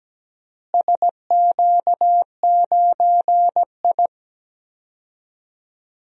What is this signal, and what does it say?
Morse code, heard as a keyed tone, about 17 wpm, 700 Hz. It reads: SQ9I
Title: DXCC-100 Call Signs - 4 Characters - Repeated 3x - 17wpm